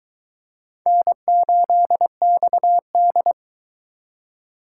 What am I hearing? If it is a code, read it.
Morse code, heard as a keyed tone, about 22 wpm, 700 Hz. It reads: N8XD